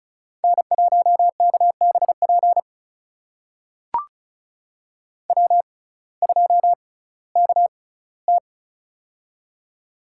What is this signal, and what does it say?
Morse code, heard as a keyed tone, about 34 wpm, spaced as for 19 wpm, 700 Hz. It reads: N1KBP W2KT
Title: USA Call Signs - 6x Character Spacing - 35wpm